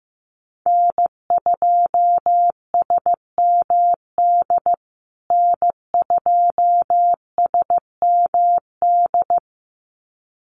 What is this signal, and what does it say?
Morse code, heard as a keyed tone, about 15 wpm, 700 Hz. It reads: N2SMD N2SMD
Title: USA Call Signs Repeated 2x 15wpm